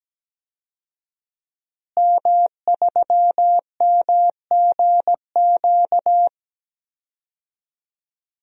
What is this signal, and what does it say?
Morse code, heard as a keyed tone, about 17 wpm, 700 Hz. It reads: M3MGQ
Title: DXCC-100 Call Signs - 5 Characters - Repeated 3x - 17wpm